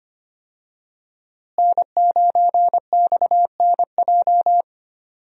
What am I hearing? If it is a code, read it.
Morse code, heard as a keyed tone, about 25 wpm, 700 Hz. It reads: N9XNJ
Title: USA Call Signs - 5 Characters - Repeated 3x - 25wpm